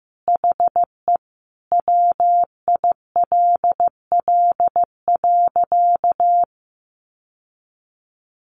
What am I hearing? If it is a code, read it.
Morse code, heard as a keyed tone, about 15 wpm, 700 Hz. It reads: HE WILL.
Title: Sentences from Top 1500 Words 15wpm